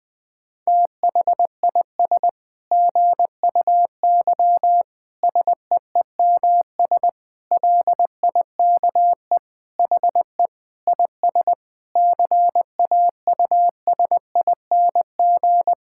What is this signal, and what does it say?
Morse code, heard as a keyed tone, about 20 wpm, 700 Hz. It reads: THIS GUY SEEMS LIKE HE IS CAUSING
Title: Sentences with 9-Word Phrases - 20wpm